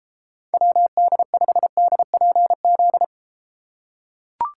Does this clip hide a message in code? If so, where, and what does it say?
Yes: Morse code from 0.5 to 3.0 s, heard as a keyed tone, about 33 wpm, 700 Hz. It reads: WD5DPZ